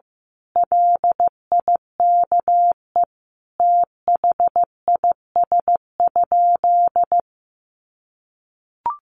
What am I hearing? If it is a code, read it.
Morse code, heard as a keyed tone, about 15 wpm, 700 Hz. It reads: LIKE THIS?
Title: Sentences from Top 1400 Words 15wpm